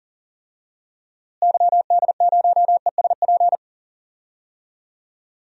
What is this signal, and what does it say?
Morse code, heard as a keyed tone, about 40 wpm, 700 Hz. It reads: YD0ESP
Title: DXCC-100 Call Signs - 6 Characters - Repeated 2x - 40wpm